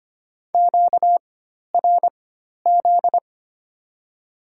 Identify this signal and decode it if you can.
Morse code, heard as a keyed tone, about 25 wpm, spaced as for 12 wpm, 700 Hz. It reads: QRZ
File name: Top 90 QSO Elements - 4x Character Spacing - 25wpm